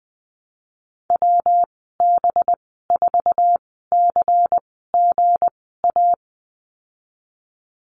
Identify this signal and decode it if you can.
Morse code, heard as a keyed tone, about 19 wpm, spaced as for 14 wpm, 700 Hz. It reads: WB4CGA
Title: 193 Lesson 31g - Letter v - Call Signs 20wpm